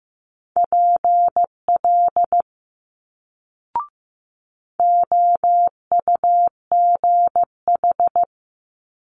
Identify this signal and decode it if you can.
Morse code, heard as a keyed tone, about 15 wpm, 700 Hz. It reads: PL OUGH
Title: Multi-letter Phonemes Quick Fire 15wpm